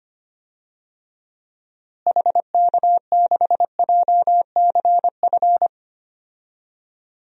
Morse code, heard as a keyed tone, about 25 wpm, 700 Hz. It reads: HK6JCF
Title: DXCC-100 Call Signs - 6 Characters - Repeated 2x - 25wpm